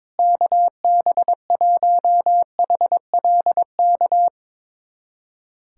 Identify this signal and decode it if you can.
Morse code, heard as a keyed tone, about 22 wpm, 700 Hz. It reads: KB1HLK
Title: USA Callsign DE Callsign - 6 Characters - 22wpm